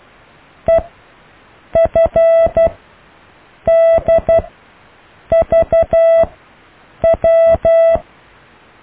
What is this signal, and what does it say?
Morse code, heard as a keyed tone, about 12 wpm, spaced as for 7 wpm, 660 Hz. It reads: EFDVW
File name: cw / IJLUM+CDRO+EKTPV+BFHNW